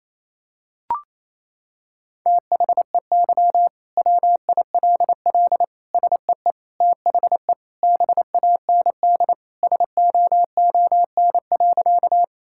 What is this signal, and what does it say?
Morse code, heard as a keyed tone, about 28 wpm, 700 Hz. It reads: THEY WILL SEE THE BAND SOON.
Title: Sentences from Top 1600 Words 28wpm